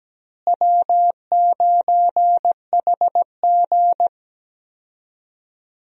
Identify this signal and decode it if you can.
Morse code, heard as a keyed tone, about 17 wpm, 700 Hz. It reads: W9HG
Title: USA Call Signs - 4 Characters - Repeated 3x - 17wpm